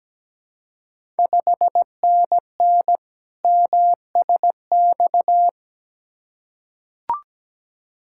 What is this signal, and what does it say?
Morse code, heard as a keyed tone, about 17 wpm, 700 Hz. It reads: 5NN MSX
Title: Ontario QSO Party Exchange 17wpm